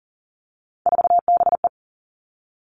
Morse code, heard as a keyed tone, about 41 wpm, 700 Hz. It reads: 4BE
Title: Three Challenging Letter-Number Combinations 40wpm